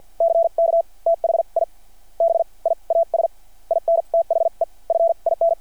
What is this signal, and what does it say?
Morse code, heard as a keyed tone, about 52 wpm, spaced as for 39 wpm, 660 Hz. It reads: CK THI BIAS IN THE FIN
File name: archived / CRSnet